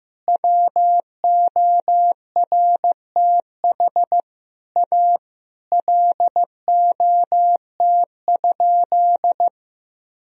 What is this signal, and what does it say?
Morse code, heard as a keyed tone, about 15 wpm, 700 Hz. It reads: WORTH A LOT?
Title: Sentences from Top 1000 Words - Review Part 4 - 15wpm